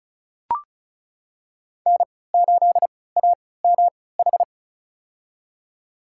Morse code, heard as a keyed tone, about 34 wpm, spaced as for 20 wpm, 700 Hz. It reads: N8AMH
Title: USA Call Signs - 5 Characters - 3x Character Spacing - 35wpm